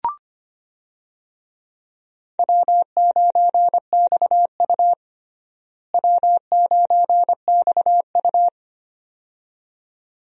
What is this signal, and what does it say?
Morse code, heard as a keyed tone, about 25 wpm, 700 Hz. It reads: W9XU W9XU
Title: USA Call Signs - 4 Characters - Repeated 2x - 25wpm